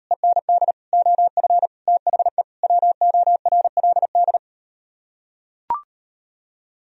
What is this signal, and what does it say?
Morse code, heard as a keyed tone, about 38 wpm, 700 Hz. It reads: END OF THE WORLD